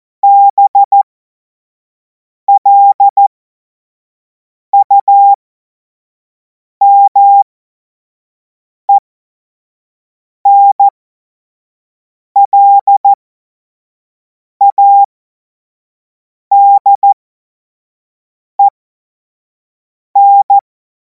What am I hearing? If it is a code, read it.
Morse code, heard as a keyed tone, about 14 wpm, spaced as for 5 wpm, 800 Hz. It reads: BLUMENLADEN
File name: Morsekurs / Witze / Witz 03. März 2026 25 F70